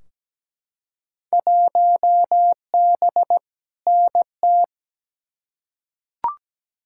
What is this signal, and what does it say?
Morse code, heard as a keyed tone, about 17 wpm, 700 Hz. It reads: 1B NT
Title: Field Day Exchanges Encore - 17wpm